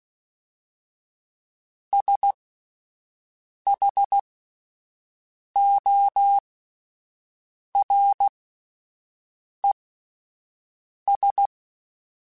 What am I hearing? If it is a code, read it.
Morse code, heard as a keyed tone, about 16 wpm, spaced as for 5 wpm, 800 Hz. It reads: SHORES